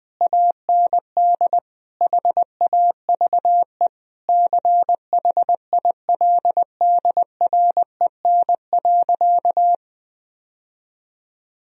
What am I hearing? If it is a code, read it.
Morse code, heard as a keyed tone, about 21 wpm, 700 Hz. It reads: AND HAVE CHILDREN.
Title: Sentences from Top 1000 Words - Review Part 1 - 20wpm